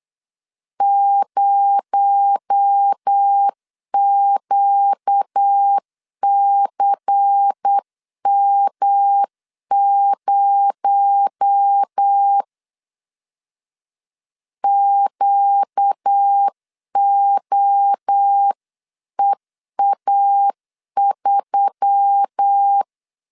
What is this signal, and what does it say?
Morse code, heard as a keyed tone, about 8 wpm, 800 Hz. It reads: 0QCM0 QOEA3
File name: CW / DET / 1 6 1500C S C 8wpm 5219